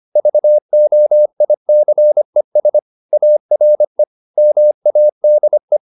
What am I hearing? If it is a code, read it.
Morse code, heard as a keyed tone, about 25 wpm, 600 Hz. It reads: VOICES ARE MADE
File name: the old man and the sea part 1 25wpm